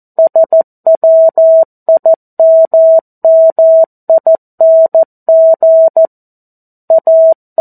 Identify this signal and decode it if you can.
Morse code, heard as a keyed tone, about 14 wpm, 650 Hz. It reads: SWIMMING A
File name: 02-Chapter-02 14 WPM